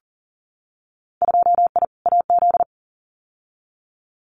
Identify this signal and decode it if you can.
Morse code, heard as a keyed tone, about 40 wpm, 700 Hz. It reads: 2I AZ
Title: Winter Field Day Exchanges 40wpm